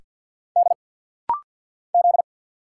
Morse code, heard as a keyed tone, about 49 wpm, 700 Hz. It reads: D Z
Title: Single Letter-Number Alt - Rapid-Fire 50wpm